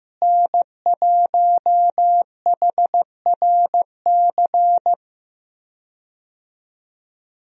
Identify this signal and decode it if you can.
Morse code, heard as a keyed tone, about 15 wpm, 700 Hz. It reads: N1HRC
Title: USA Call Signs - 5 Characters - Repeated 3x - 15wpm